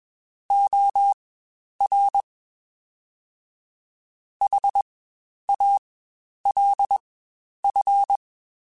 Morse code, heard as a keyed tone, about 20 wpm, spaced as for 10 wpm, 800 Hz. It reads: OR HALF
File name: Spaced / Chuckles Spaced-mp3s / HalfFull sp @20